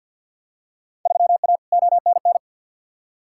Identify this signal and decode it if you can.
Morse code, heard as a keyed tone, about 50 wpm, 700 Hz. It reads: 3A ONN
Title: Field Day Exchanges Encore - 50wpm